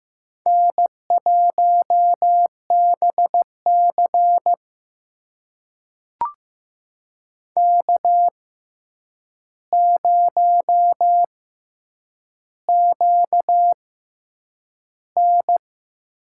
Morse code, heard as a keyed tone, about 15 wpm, spaced as for 8 wpm, 700 Hz. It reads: N1BC K0QN
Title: USA Call Signs - 4 Characters - 6x Character Spacing - 15wpm